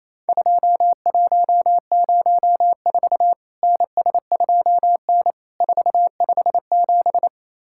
Morse code, heard as a keyed tone, about 28 wpm, 700 Hz. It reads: 2104 NS2N 457
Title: Worked All Europe DX Contest - QTC Exchange - 28wpm